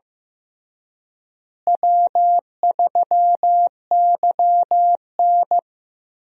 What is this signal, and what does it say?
Morse code, heard as a keyed tone, about 15 wpm, 700 Hz. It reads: W3YN
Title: USA Call Signs - 4 Characters - Repeated 3x - 15wpm